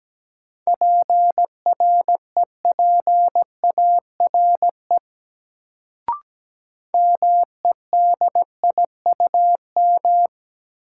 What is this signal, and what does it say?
Morse code, heard as a keyed tone, about 17 wpm, 700 Hz. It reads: PREPARE MEDIUM
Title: Sentences from Top 900 Words - New Word Drill - 17wpm